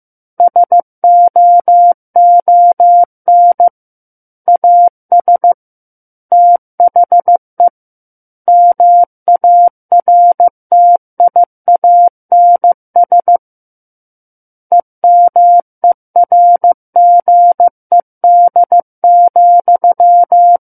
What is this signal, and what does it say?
Morse code, heard as a keyed tone, about 15 wpm, 700 Hz. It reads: SOON AS THE MARTIANS EMERGED,
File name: Book One / 06-Chapter 6 - 15 WPM